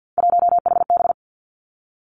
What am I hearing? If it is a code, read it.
Morse code, heard as a keyed tone, about 50 wpm, 700 Hz. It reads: 1HB
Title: Three Challenging Letter-Number Combinations 50wpm